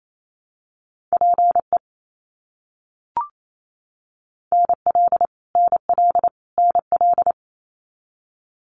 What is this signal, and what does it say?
Morse code, heard as a keyed tone, about 28 wpm, 700 Hz. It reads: PE NL NL NL
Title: Canadian Provinces and Territories - Repeated 3x - 28wpm